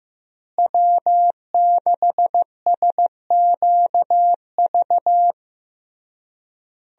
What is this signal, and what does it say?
Morse code, heard as a keyed tone, about 16 wpm, 700 Hz. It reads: W6SQV
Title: USA Call Signs - 5 Characters - Repeated 3x - 15wpm